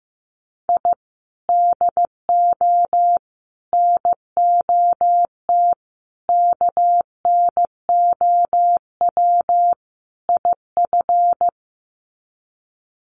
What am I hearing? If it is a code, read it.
Morse code, heard as a keyed tone, about 15 wpm, 700 Hz. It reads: I DO NOT KNOW IF
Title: Sets of 5 Words 15wpm